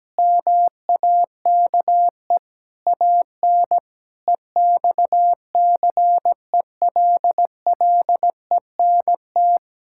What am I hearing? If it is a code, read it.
Morse code, heard as a keyed tone, about 17 wpm, 700 Hz. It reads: MAKE AN EXCELLENT
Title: Sentences from Top 1000 Words - Review Part 5 - 17wpm